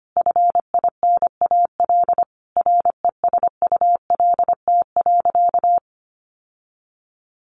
Morse code, heard as a keyed tone, about 25 wpm, 700 Hz. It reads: FINAL RESULT.
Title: Sentences from Top 800 Words 25wpm